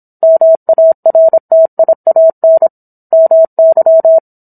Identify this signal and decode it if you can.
Morse code, heard as a keyed tone, about 27 wpm, 650 Hz. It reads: MARTIAN MY